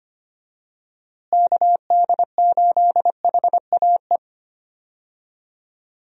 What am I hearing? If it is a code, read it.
Morse code, heard as a keyed tone, about 25 wpm, 700 Hz. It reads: KD8HAE
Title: USA Call Signs - 6 Characters - Repeated 3x - 25wpm